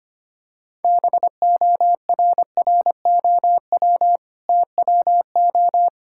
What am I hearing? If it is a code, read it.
Morse code, heard as a keyed tone, about 25 wpm, 690 Hz. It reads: BORROW TWO
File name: Easy Sentences with Hints - 25wpm